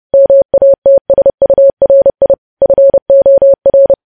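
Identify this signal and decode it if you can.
Morse code, heard as a keyed tone, about 30 wpm, 550 Hz. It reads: MATSURI FOR
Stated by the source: Morse news headlines sent at 30 wpm, opening with vvv.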